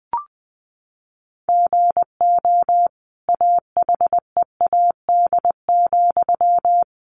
Morse code, heard as a keyed tone, about 20 wpm, 700 Hz. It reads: GO AHEAD,
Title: Sentences from Top 1100 Words 20wpm